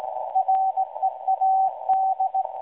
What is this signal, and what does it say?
Morse code, heard as a keyed tone, about 17 wpm, 760 Hz. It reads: READ
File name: WebSDR records / SAQ at PI4THT msg 0800-0804 17.2kHz